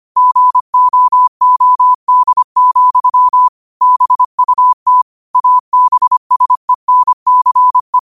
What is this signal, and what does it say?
Morse code, heard as a keyed tone, about 25 wpm, 1000 Hz. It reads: GOOD, BUT ABSENCE
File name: qotdfiles / 25wpm